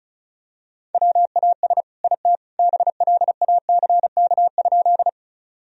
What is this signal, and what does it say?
Morse code, heard as a keyed tone, about 34 wpm, 700 Hz. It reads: WAS IT BLACK?
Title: Sentences from Top 1000 Words - Review Part 2 - 35wpm